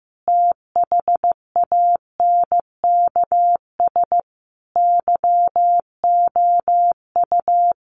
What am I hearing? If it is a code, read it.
Morse code, heard as a keyed tone, about 15 wpm, 700 Hz. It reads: THANKS YOU
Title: Sentences from Top 1200 Words 15wpm